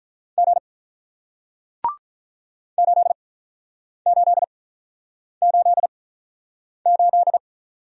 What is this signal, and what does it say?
Morse code, heard as a keyed tone, about 47 wpm, spaced as for 9 wpm, 700 Hz. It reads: G 8888